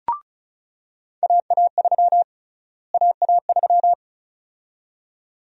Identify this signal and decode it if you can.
Morse code, heard as a keyed tone, about 36 wpm, 700 Hz. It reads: AA3 AA3
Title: USA Call Sign Prefixes - Repeated 2x - 35wpm